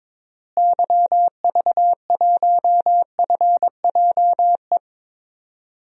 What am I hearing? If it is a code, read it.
Morse code, heard as a keyed tone, about 21 wpm, 700 Hz. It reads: YV1FJE
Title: DXCC-100 Call Signs - 6 Characters - Repeated 3x - 22wpm